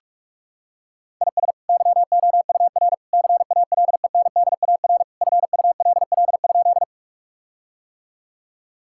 Morse code, heard as a keyed tone, about 46 wpm, 700 Hz. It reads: IS YOUR CALENDAR FULL?
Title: Sentences from Top 1600 Words 45wpm